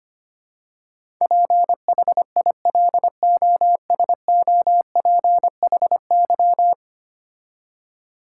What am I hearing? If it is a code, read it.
Morse code, heard as a keyed tone, about 25 wpm, 700 Hz. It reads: PHILOSOPHY
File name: 10 Letter Words - First 3 Letters Given - 25wpm